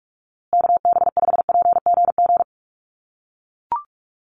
Morse code, heard as a keyed tone, about 44 wpm, 700 Hz. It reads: KB5PGG
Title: USA Call Signs Repeated 2x 45wpm